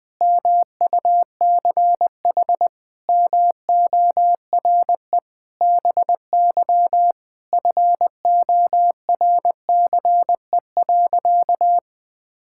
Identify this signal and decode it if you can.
Morse code, heard as a keyed tone, about 20 wpm, 700 Hz. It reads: MUCH MORE BY FORCE.
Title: Sentences from Top 1000 Words - Review Part 6 - 20wpm